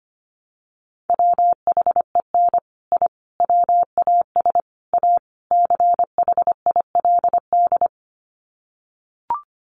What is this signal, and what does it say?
Morse code, heard as a keyed tone, about 25 wpm, 700 Hz. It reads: WHEN I WAS A CHILD
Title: Sets of 5 Words Encore 25wpm